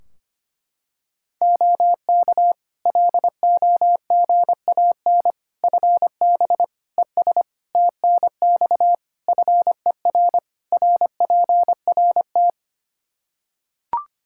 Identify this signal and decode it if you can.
Morse code, heard as a keyed tone, about 25 wpm, 700 Hz. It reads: OK LOGAN FB ES TNX FER RPRT
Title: Practice QSO - Exchange 2 - Info - 25wpm